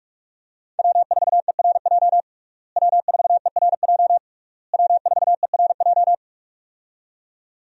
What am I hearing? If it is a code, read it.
Morse code, heard as a keyed tone, about 42 wpm, 700 Hz. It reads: W4ERJ W4ERJ W4ERJ
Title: USA Call Signs - 5 Characters - Repeated 3x - 45wpm